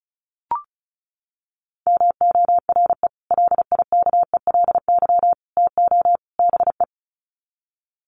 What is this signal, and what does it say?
Morse code, heard as a keyed tone, about 34 wpm, 700 Hz. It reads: MORE LIKELY TO BE
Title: Sets of 4 Words 35wpm